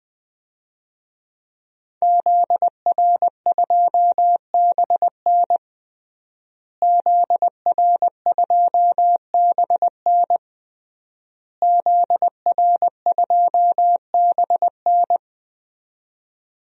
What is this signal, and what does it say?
Morse code, heard as a keyed tone, about 21 wpm, 700 Hz. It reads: ZR2BN ZR2BN ZR2BN
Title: DXCC-100 Call Signs - 5 Characters - Repeated 3x - 20wpm